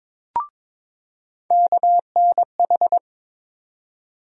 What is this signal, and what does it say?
Morse code, heard as a keyed tone, about 22 wpm, 700 Hz. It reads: KNH
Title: USA Call Sign Postfixes - 3 Letters 22wpm